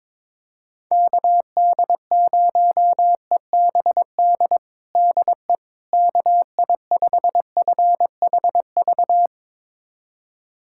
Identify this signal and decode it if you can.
Morse code, heard as a keyed tone, about 22 wpm, 700 Hz. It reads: KD0EBD DE KI5FHV
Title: USA Callsign DE Callsign - 6 Characters - 22wpm